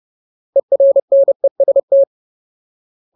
Morse code, heard as a keyed tone, about 30 wpm, 550 Hz. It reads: ERNEST
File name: Ultimate Word List / 30WPM Ultimate